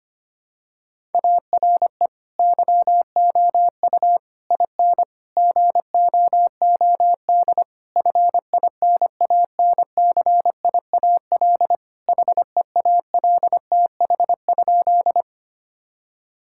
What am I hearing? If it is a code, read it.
Morse code, heard as a keyed tone, about 25 wpm, 700 Hz. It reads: ARE YOU IN GOOD FINANCIAL HEALTH?
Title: Sentences from Top 1000 Words - Review Part 5 - 25wpm